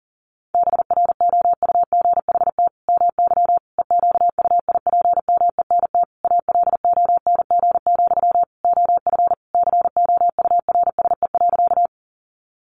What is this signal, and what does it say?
Morse code, heard as a keyed tone, about 40 wpm, 700 Hz. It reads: BROUGHT MY EQUIPMENT ALONG, OF COURSE.
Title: Sentences from Top 700 Words 40wpm